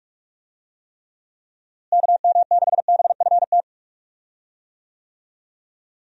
Morse code, heard as a keyed tone, about 47 wpm, 700 Hz. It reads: KM6BFT
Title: USA Call Signs - 6 Characters - Repeated 3x - 45wpm